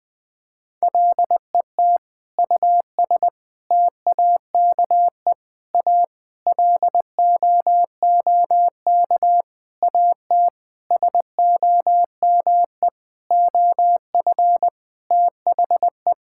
Morse code, heard as a keyed tone, about 20 wpm, 700 Hz. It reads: LET US TAKE A LOOK AT SOME OF THE
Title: Sentences with 8-Word Phrases - 20wpm